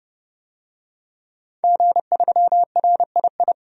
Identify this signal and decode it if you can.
Morse code, heard as a keyed tone, about 30 wpm, 700 Hz. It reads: G3RII
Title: DXCC-100 Call Signs - 5 Characters - 30wpm